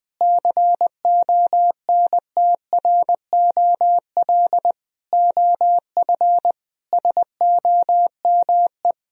Morse code, heard as a keyed tone, about 20 wpm, 700 Hz. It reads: CONTROL OF SOME
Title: Sentences from Top 1000 Words - Review Part 3 - 20wpm